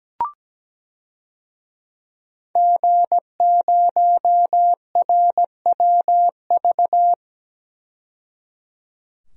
Morse code, heard as a keyed tone, about 17 wpm, 700 Hz. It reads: G0RWV
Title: DXCC-100 Call Signs - 5 Characters - 17wpm